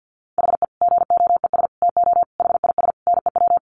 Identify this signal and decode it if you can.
Morse code, heard as a keyed tone, about 48 wpm, 700 Hz. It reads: HE GOES TO HIS NEW
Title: Easy Phrases 50wpm